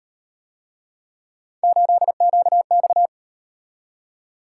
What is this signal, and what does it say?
Morse code, heard as a keyed tone, about 37 wpm, 700 Hz. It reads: 8QX